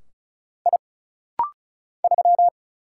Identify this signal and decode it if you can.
Morse code, heard as a keyed tone, about 36 wpm, 700 Hz. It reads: I 3
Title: Single Letter-Number - Rapid-Fire 35wpm